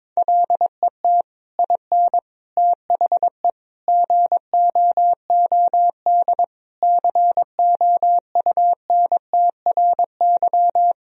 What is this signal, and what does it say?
Morse code, heard as a keyed tone, about 22 wpm, 700 Hz. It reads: LET IN THE GOOD COUNTRY
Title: Sentences from Top 1000 Words - Review Part 1 - 22wpm